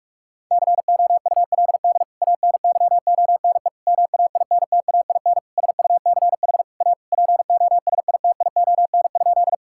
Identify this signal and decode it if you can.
Morse code, heard as a keyed tone, about 41 wpm, 700 Hz. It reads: COULD ANYONE MAINTAIN SUCH A POSITION?
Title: Sentences from Top 1000 Words - Review Part 5 - 45wpm